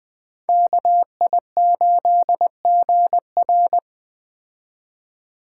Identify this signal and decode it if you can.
Morse code, heard as a keyed tone, about 21 wpm, 700 Hz. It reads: KI8GR